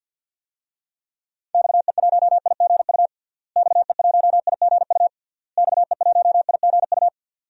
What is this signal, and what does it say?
Morse code, heard as a keyed tone, about 46 wpm, 700 Hz. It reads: XE1IGU XE1IGU XE1IGU
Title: DXCC-100 Call Signs - 6 Characters - Repeated 3x - 50wpm